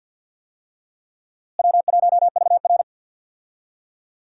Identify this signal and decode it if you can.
Morse code, heard as a keyed tone, about 52 wpm, 700 Hz. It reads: W1VR
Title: USA Call Signs - 4 Characters - Repeated 2x - 50wpm